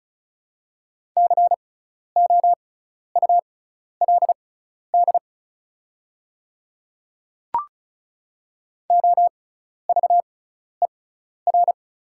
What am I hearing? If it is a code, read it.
Morse code, heard as a keyed tone, about 34 wpm, spaced as for 12 wpm, 700 Hz. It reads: COULD OVER
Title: 6x Character Spacing - Top 100 Words - 35wpm